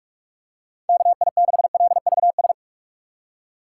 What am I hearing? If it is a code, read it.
Morse code, heard as a keyed tone, about 46 wpm, 700 Hz. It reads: KI6LVS
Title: USA Call Signs - 6 Characters - 4x Character Spacing - 45wpm